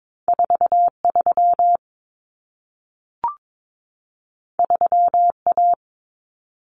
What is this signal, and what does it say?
Morse code, heard as a keyed tone, about 22 wpm, 700 Hz. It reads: 43 3A
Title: Common Cut Numbers - 2-Digit 22wpm